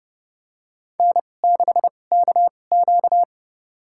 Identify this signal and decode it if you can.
Morse code, heard as a keyed tone, about 30 wpm, spaced as for 22 wpm, 700 Hz. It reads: N6KQ
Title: USA Call Signs - 4 Characters - 2x Character Spacing - 30wpm